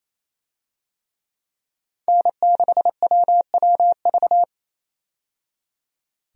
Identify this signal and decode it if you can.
Morse code, heard as a keyed tone, about 28 wpm, 700 Hz. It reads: N6WWV
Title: USA Call Signs - 5 Characters - 28wpm